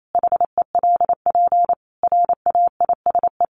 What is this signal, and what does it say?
Morse code, heard as a keyed tone, about 28 wpm, 700 Hz. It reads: HELP RAISE
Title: Sentences from Top 1400 Words 28wpm